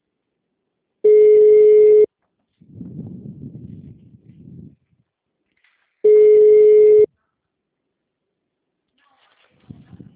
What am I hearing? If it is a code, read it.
Ringback tone (Europe, 425 Hz).
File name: out- -2009- - - .96262